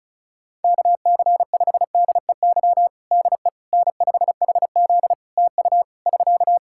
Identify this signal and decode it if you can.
Morse code, heard as a keyed tone, about 36 wpm, 700 Hz. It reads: KC5DEY DE N5HZ TU <SK>
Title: Practice QSO - Exchange 4 - Ending - 35wpm